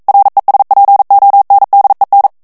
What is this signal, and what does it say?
Morse code, heard as a keyed tone, about 42 wpm, 800 Hz. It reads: RESPONDEN
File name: morse mp3 / ESPAGNOL TEXTE EN CLAIR 1 VITESSE 40 MOTS MINUTES